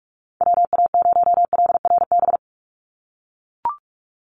Morse code, heard as a keyed tone, about 44 wpm, 700 Hz. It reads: WA0LRB
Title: USA Call Signs 45wpm